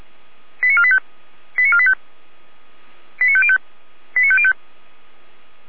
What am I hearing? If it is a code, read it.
Five-tone selective call (ZVEI), heard as quick runs of five tones, tones of about 70 ms. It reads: 89485 89485 89505 89505